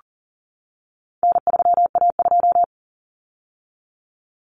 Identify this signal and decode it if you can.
Morse code, heard as a keyed tone, about 40 wpm, 700 Hz. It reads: N3A2
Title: Common Cut Numbers - 4-Digit 40wpm